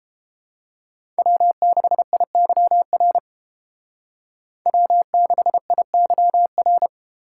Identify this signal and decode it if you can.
Morse code, heard as a keyed tone, about 30 wpm, 700 Hz. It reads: W6IYR W6IYR